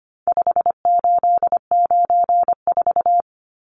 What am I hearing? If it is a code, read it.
Morse code, heard as a keyed tone, about 25 wpm, 700 Hz. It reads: 5894
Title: Numbers - 4-Digit 25wpm